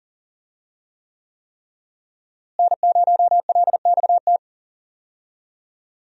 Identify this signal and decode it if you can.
Morse code, heard as a keyed tone, about 40 wpm, 700 Hz. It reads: N0LXT
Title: USA Call Signs - 5 Characters - Repeated 3x - 40wpm